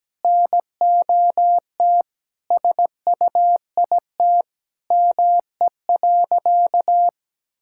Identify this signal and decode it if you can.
Morse code, heard as a keyed tone, about 17 wpm, 700 Hz. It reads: NOT SUIT ME.
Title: Sentences from Top 1600 Words 17wpm